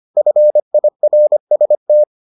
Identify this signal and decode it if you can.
Morse code, heard as a keyed tone, about 25 wpm, 600 Hz. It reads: FIRST